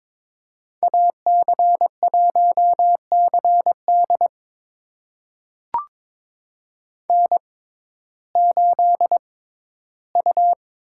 Morse code, heard as a keyed tone, about 22 wpm, spaced as for 14 wpm, 700 Hz. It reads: AC1CD N8U
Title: USA Call Signs - 5 Characters - 6x Character Spacing - 22wpm